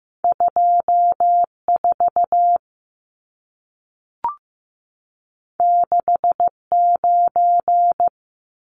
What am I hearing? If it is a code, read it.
Morse code, heard as a keyed tone, about 15 wpm, 700 Hz. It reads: 24 69
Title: Numbers - 2-Digit 15wpm